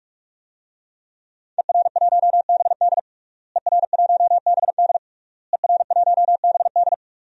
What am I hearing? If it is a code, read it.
Morse code, heard as a keyed tone, about 46 wpm, 700 Hz. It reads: ER1BD ER1BD ER1BD
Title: DXCC-100 Call Signs - 5 Characters - Repeated 3x - 45wpm